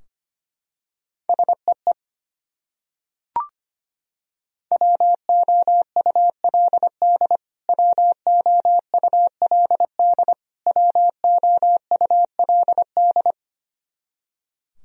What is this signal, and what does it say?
Morse code, heard as a keyed tone, about 25 wpm, 700 Hz. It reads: SEE WOULD WOULD WOULD
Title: Top 100 Words Repeated 25wpm